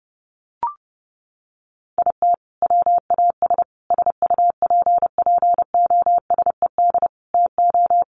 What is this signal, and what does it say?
Morse code, heard as a keyed tone, about 30 wpm, 700 Hz. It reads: IT WAS SUPPOSED TO
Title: Sets of 5 Words Encore 30wpm